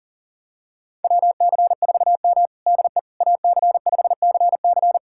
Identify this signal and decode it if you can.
Morse code, heard as a keyed tone, about 40 wpm, 700 Hz. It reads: WC4M DE AC5CC
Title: USA Callsign DE Callsign 40wpm